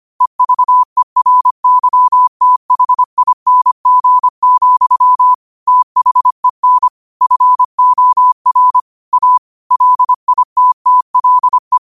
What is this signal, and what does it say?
Morse code, heard as a keyed tone, about 25 wpm, 1000 Hz. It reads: EVERYTHING, THEN FOR A LITTLE
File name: qotdfiles / 25wpm